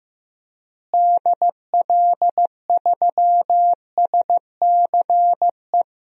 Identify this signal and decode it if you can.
Morse code, heard as a keyed tone, about 15 wpm, 700 Hz. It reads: DL3SCE
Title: DXCC-100 Call Signs - 6 Characters - Repeated 3x - 15wpm